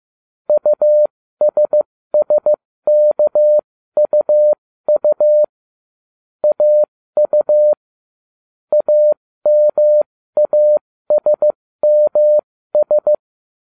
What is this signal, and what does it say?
Morse code, heard as a keyed tone, about 15 wpm, spaced as for 13 wpm, 600 Hz. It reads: USSKUU AU AMASMS